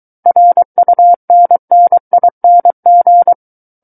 Morse code, heard as a keyed tone, about 23 wpm, 700 Hz. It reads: RUNNING
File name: Book One / 14-Chapter 14 - 23 WPM